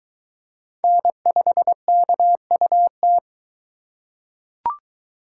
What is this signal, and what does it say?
Morse code, heard as a keyed tone, about 22 wpm, 700 Hz. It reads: N5KUT